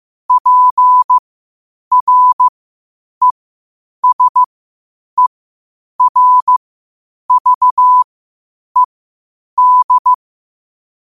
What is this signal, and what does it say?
Morse code, heard as a keyed tone, about 15 wpm, spaced as for 8 wpm, 1000 Hz. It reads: PRESERVED
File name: qotdfiles / 8wpm